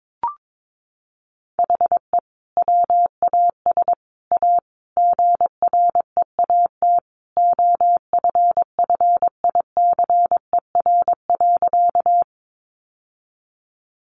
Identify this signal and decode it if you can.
Morse code, heard as a keyed tone, about 22 wpm, 700 Hz. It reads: HE WAS A GREAT OFFICER.
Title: Sentences from Top 1200 Words 22wpm